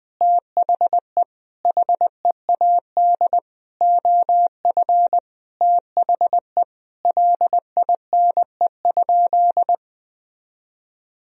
Morse code, heard as a keyed tone, about 21 wpm, 700 Hz. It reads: THE HEAD OF THE LINE?
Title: Sentences from Top 1000 Words - Review Part 1 - 20wpm